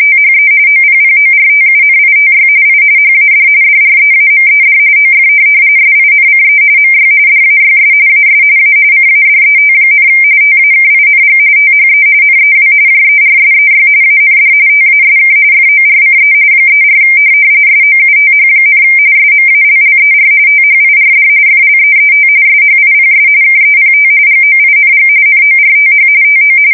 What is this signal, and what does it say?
RTTY, about 45.45 baud. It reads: THE OPPORTUNITY I HAD DREAMED OF SINCE MY DAYS AS A CADET, TAKING THE NEW JERSEY AND TASK FORCE 34 SOUTH, SMALLER THIRD FLEET SHIPS KEPT GOING NORTH TO
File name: fsk170 / USS-New-Jersey